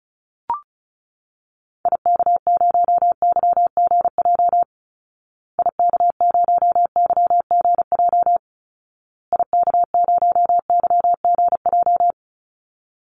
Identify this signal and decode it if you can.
Morse code, heard as a keyed tone, about 34 wpm, 700 Hz. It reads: IK0YGJ IK0YGJ IK0YGJ
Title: Call Signs Repeated 2x - 35wpm